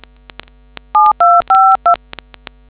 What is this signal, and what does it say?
DTMF, heard as telephone touch tones, tones of about 210 ms. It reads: 7252